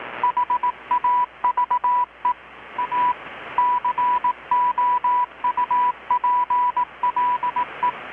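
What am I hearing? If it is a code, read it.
Morse code, heard as a keyed tone, about 18 wpm, 980 Hz, with radio noise. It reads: HAVE A COUPLE
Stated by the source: Morse from KPH's Night of Nights XX maritime radiotelegraphy event, received in Chiapas, Mexico about 3,800 km away on a Kenwood R-600 with a grounded 10 m long-wire antenna.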